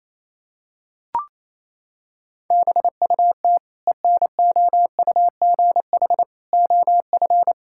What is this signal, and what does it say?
Morse code, heard as a keyed tone, about 28 wpm, 700 Hz. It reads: BUT ENOUGH OF
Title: Sentences from Top 1000 Words - Review Part 5 - 28wpm